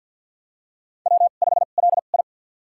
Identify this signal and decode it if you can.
Morse code, heard as a keyed tone, about 51 wpm, spaced as for 36 wpm, 700 Hz. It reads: W5LI
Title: USA Call Signs - 4 Characters - 2x Character Spacing - 50wpm